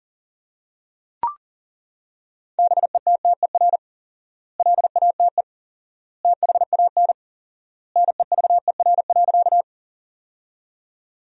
Morse code, heard as a keyed tone, about 40 wpm, 700 Hz. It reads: BETTER LATE THAN NEVER.
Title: 4x Word-Spacing - Sentences from Top 400 Words - 40wpm